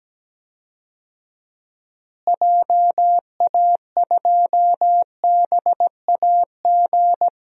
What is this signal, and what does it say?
Morse code, heard as a keyed tone, about 17 wpm, 700 Hz. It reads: JA2BAG
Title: DXCC-100 Call Signs - 6 Characters - Repeated 3x - 17wpm